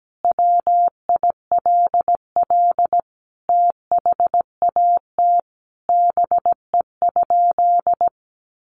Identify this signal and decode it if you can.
Morse code, heard as a keyed tone, about 17 wpm, 700 Hz. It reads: WILL THAT BE?
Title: Sentences from Top 100 Words 17wpm